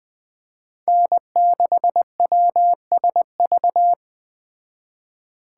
Morse code, heard as a keyed tone, about 20 wpm, 700 Hz. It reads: N6WSV
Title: Speed Racing - USA 1x3 Call Signs 17wpm